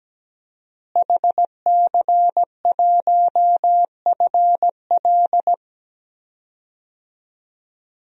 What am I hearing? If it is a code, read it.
Morse code, heard as a keyed tone, about 17 wpm, 700 Hz. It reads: HC1FL
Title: DXCC-100 Call Signs - 5 Characters - Repeated 2x - 17wpm